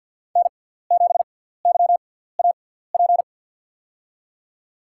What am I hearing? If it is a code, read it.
Morse code, heard as a keyed tone, about 53 wpm, spaced as for 17 wpm, 700 Hz. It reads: N7YAP
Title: USA Call Signs - 5 Characters - 6x Character Spacing - 50wpm